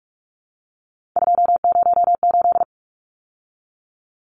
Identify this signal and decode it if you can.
Morse code, heard as a keyed tone, about 44 wpm, 700 Hz. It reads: 208
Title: Numbers - 3-Digit 45wpm